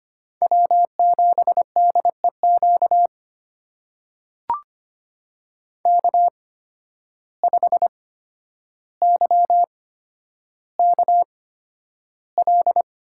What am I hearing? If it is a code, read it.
Morse code, heard as a keyed tone, about 25 wpm, spaced as for 11 wpm, 700 Hz. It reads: W7DEQ K5YKL
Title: USA Call Signs - 5 Characters - 8x Character Spacing - 25wpm